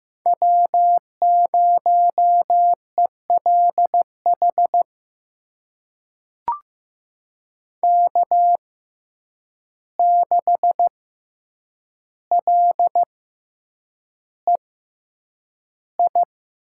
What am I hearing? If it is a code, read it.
Morse code, heard as a keyed tone, about 15 wpm, spaced as for 8 wpm, 700 Hz. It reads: W0ELH K6LEI
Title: USA Call Signs - 5 Characters - 6x Character Spacing - 15wpm